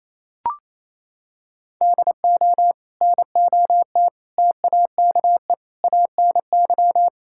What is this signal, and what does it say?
Morse code, heard as a keyed tone, about 28 wpm, 700 Hz. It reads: DO NOT TAKE ANY
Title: Sentences from Top 1000 Words - Review Part 5 - 28wpm